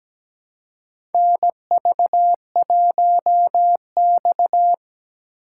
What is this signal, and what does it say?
Morse code, heard as a keyed tone, about 17 wpm, 700 Hz. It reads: NV1X
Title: USA Call Signs - 4 Characters - 3x Character Spacing - 17wpm